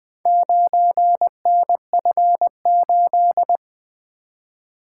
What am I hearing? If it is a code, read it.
Morse code, heard as a keyed tone, about 20 wpm, 700 Hz. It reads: 9NF8
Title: Four Letter-Number Combinations - 20wpm